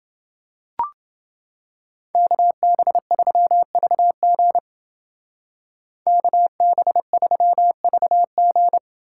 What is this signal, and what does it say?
Morse code, heard as a keyed tone, about 29 wpm, 700 Hz. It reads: KB3VG KB3VG